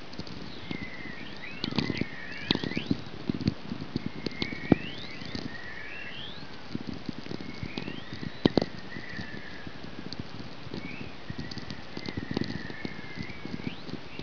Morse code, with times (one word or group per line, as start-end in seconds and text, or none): none